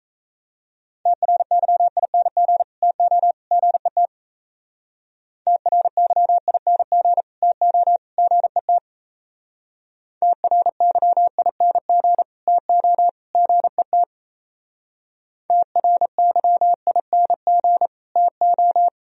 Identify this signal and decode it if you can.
Morse code, heard as a keyed tone, about 36 wpm, 700 Hz. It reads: TRYING TO GET TRYING TO GET TRYING TO GET TRYING TO